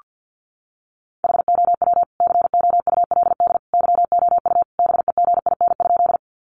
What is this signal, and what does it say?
Morse code, heard as a keyed tone, about 48 wpm, 700 Hz. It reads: HOW COULD YOU BEGIN?
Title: Sentences from Top 300 Words 50wpm